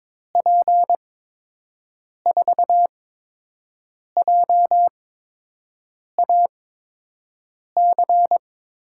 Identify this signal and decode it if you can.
Morse code, heard as a keyed tone, about 22 wpm, spaced as for 6 wpm, 700 Hz. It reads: P4JAC
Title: USA Call Signs - 6 Characters - 8x Character Spacing - 22wpm